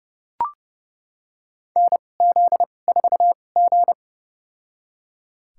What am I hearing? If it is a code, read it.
Morse code, heard as a keyed tone, about 32 wpm, spaced as for 22 wpm, 700 Hz. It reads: NZ4G
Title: USA Call Signs - 4 Characters - 2x Character Spacing - 30wpm